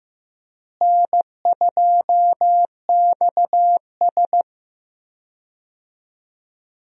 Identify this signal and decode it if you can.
Morse code, heard as a keyed tone, about 15 wpm, 700 Hz. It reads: N2XS
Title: USA Call Signs - 4 Characters - Repeated 2x - 15wpm